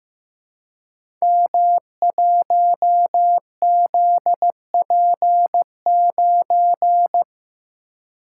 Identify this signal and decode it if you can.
Morse code, heard as a keyed tone, about 15 wpm, 700 Hz. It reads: M1ZP9
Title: Code Groups - 5 Letter:Numbers - 15wpm